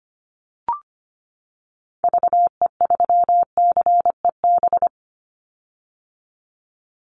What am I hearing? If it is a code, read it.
Morse code, heard as a keyed tone, about 25 wpm, 700 Hz. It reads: VE3CEB
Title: Call Signs - 25wpm